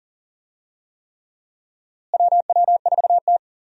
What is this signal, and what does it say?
Morse code, heard as a keyed tone, about 38 wpm, 700 Hz. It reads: WW4T
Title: USA Call Signs - 4 Characters - Repeated 3x - 40wpm